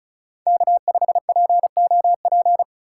Morse code, heard as a keyed tone, about 34 wpm, 700 Hz. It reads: K5POP
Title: USA Call Signs - 5 Characters - 6x Character Spacing - 35wpm